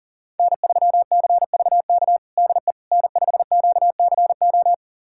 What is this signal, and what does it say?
Morse code, heard as a keyed tone, about 40 wpm, 700 Hz. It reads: N3CVK DE N5QCO
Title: USA Callsign DE Callsign - 5 Characters - 40wpm